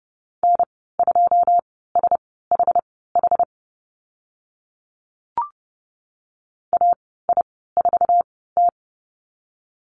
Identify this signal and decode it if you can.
Morse code, heard as a keyed tone, about 30 wpm, spaced as for 17 wpm, 700 Hz. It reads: N2SHH AI4T
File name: Lesson 14g - Number 2 - Call Signs 30wpm